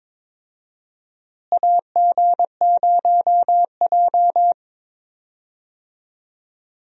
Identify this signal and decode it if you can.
Morse code, heard as a keyed tone, about 22 wpm, 700 Hz. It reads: AG0J
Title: USA Call Signs - 4 Characters - Repeated 2x - 22wpm